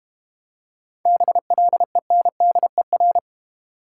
Morse code, heard as a keyed tone, about 32 wpm, 690 Hz. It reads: BLENDER